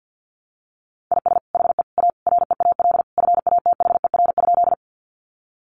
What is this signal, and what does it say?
Morse code, heard as a keyed tone, about 50 wpm, 700 Hz. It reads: IS HE A REAL FATHER?
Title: Sentences from Top 300 Words 50wpm